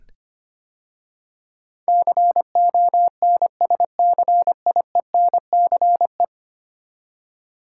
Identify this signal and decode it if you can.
Morse code, heard as a keyed tone, about 25 wpm, 700 Hz. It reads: CONSCIENCE
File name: 10 Letter Words - First 3 Letters Given - 25wpm